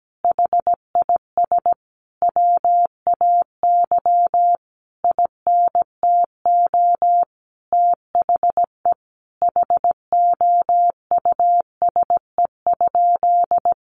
Morse code, heard as a keyed tone, about 17 wpm, 700 Hz. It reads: HIS WAY INTO THE HOUSE?
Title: Sentences from Top 1300 Words 17wpm